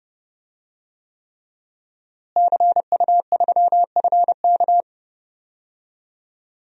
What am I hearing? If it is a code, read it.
Morse code, heard as a keyed tone, about 30 wpm, 700 Hz. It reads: CU3FK
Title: DXCC-100 Call Signs - 5 Characters - Repeated 3x - 30wpm